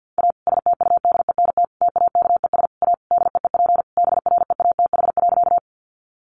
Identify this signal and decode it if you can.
Morse code, heard as a keyed tone, about 50 wpm, 700 Hz. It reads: A STUDENT TAKES A DEEP BREATH.
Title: Additional Easy Phrases 50wpm